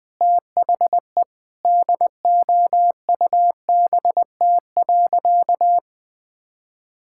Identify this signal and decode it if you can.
Morse code, heard as a keyed tone, about 20 wpm, 700 Hz. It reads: THE DOUBT.
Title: Sentences with 7-Word Phrases - 20wpm